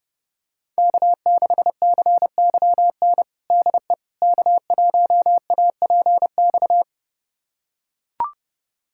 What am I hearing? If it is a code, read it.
Morse code, heard as a keyed tone, about 30 wpm, 700 Hz. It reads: K6CYN DE K1APX
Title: USA Callsign DE Callsign - 5 Characters - 30wpm